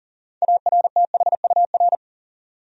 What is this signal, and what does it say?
Morse code, heard as a keyed tone, about 40 wpm, 700 Hz. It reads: ARTHUR
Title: USA Common Names - 40wpm